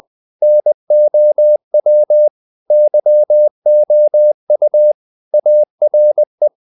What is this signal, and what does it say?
Morse code, heard as a keyed tone, about 20 wpm, 600 Hz. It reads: NOW YOU ARE
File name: the old man and the sea part 2 20wpm